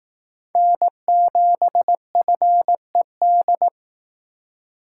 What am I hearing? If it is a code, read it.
Morse code, heard as a keyed tone, about 18 wpm, 700 Hz. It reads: N7FED
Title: Speed Racing - USA 1x3 Call Signs 15wpm